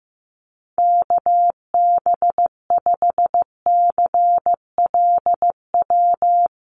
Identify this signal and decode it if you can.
Morse code, heard as a keyed tone, about 15 wpm, 700 Hz. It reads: KB5CLW
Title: USA Call Signs Repeated 2x 15wpm